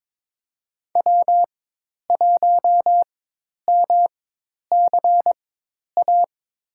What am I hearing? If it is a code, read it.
Morse code, heard as a keyed tone, about 22 wpm, spaced as for 10 wpm, 700 Hz. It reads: W1MCA
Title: USA Call Signs - 5 Characters - 4x Character Spacing - 22wpm